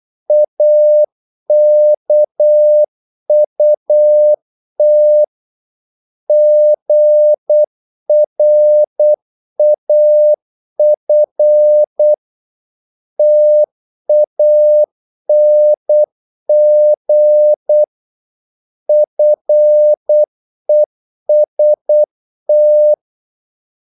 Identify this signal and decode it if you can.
Morse code, heard as a keyed tone, about 8 wpm, 600 Hz. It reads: AKUT GRAF TANG FEST
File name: Korte ord DK 8wpm